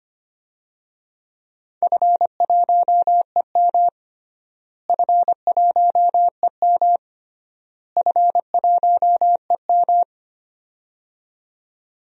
Morse code, heard as a keyed tone, about 25 wpm, 700 Hz. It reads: F1EM F1EM F1EM
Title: DXCC-100 Call Signs - 4 Characters - Repeated 3x - 25wpm